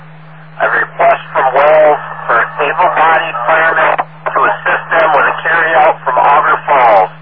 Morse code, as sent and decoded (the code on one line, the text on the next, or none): none